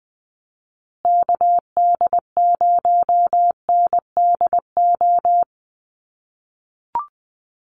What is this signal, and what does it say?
Morse code, heard as a keyed tone, about 20 wpm, 700 Hz. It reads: KD0NDO
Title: USA Call Signs Repeated 3x 20wpm